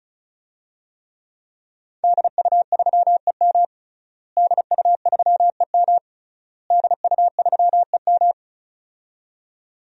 Morse code, heard as a keyed tone, about 36 wpm, 700 Hz. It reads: DU3EM DU3EM DU3EM
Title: DXCC-100 Call Signs - 5 Characters - Repeated 3x - 35wpm